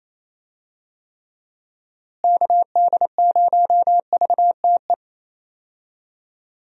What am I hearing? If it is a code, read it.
Morse code, heard as a keyed tone, about 28 wpm, 700 Hz. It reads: KD0VTE
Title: USA Call Signs - 6 Characters - Repeated 3x - 28wpm